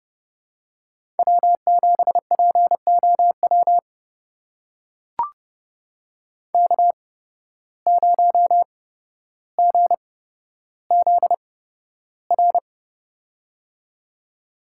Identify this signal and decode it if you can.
Morse code, heard as a keyed tone, about 30 wpm, spaced as for 13 wpm, 700 Hz. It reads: W7POW K0GZR
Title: USA Call Signs - 5 Characters - 8x Character Spacing - 30wpm